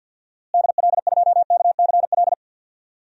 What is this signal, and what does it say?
Morse code, heard as a keyed tone, about 52 wpm, 700 Hz. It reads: DL2KCL
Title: DXCC-100 Call Signs - 6 Characters - Repeated 3x - 50wpm